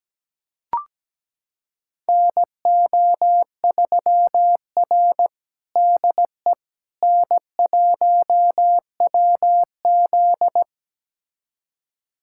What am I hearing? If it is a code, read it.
Morse code, heard as a keyed tone, about 17 wpm, 700 Hz. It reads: NO3R DE N1WZ
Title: USA Callsign DE Callsign - 4 Characters - 17wpm